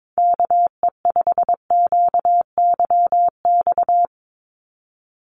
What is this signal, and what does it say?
Morse code, heard as a keyed tone, about 22 wpm, 700 Hz. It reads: KE5QYX
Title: USA Call Signs 22wpm